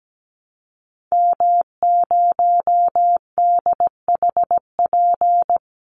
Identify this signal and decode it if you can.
Morse code, heard as a keyed tone, about 17 wpm, 700 Hz. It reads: M0DHP
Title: Call Signs - 17wpm